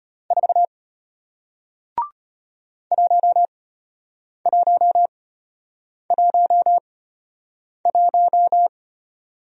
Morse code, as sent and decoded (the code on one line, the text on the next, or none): ....- / .---- .---- .---- .----
4 1111